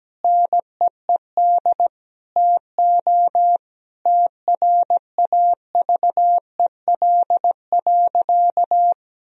Morse code, heard as a keyed tone, about 17 wpm, 700 Hz. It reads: NEED TO TRAVEL.
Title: Sentences from Top 1000 Words - Review Part 2 - 17wpm